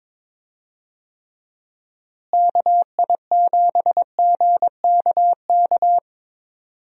Morse code, heard as a keyed tone, about 21 wpm, 700 Hz. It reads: KI7GKK
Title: USA Call Signs - 6 Characters - Repeated 3x - 22wpm